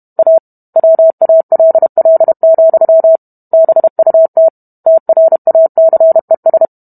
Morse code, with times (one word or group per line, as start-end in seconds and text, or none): 0.2-0.4 A
0.8-3.2 WALL,
3.5-4.5 BUT
4.9-6.7 TRACES